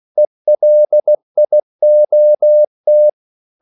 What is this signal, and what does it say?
Morse code, heard as a keyed tone, about 16 wpm, 600 Hz. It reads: ELIOT
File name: CW / Practice / EnglishQuotations0216